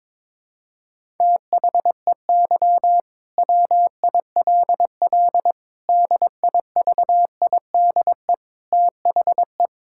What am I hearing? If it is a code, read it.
Morse code, heard as a keyed tone, about 22 wpm, 700 Hz. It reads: THEY WILL DIVIDE THE
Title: Sentences from Top 1600 Words 22wpm